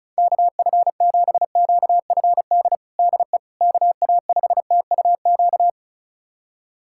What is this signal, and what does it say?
Morse code, heard as a keyed tone, about 34 wpm, 700 Hz. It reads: KF7QFD DE KA5TUQ
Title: USA Callsign DE Callsign - 6 Characters - 35wpm